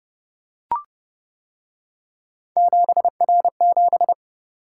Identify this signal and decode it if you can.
Morse code, heard as a keyed tone, about 30 wpm, 700 Hz. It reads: 7R7